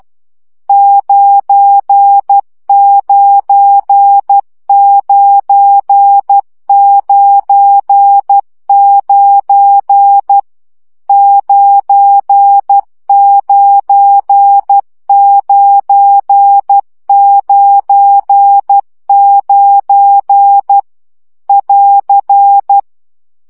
Morse code, heard as a keyed tone, about 12 wpm, 800 Hz. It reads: 99999 99999 <AR>
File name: CW gyakorlas 9 10 baud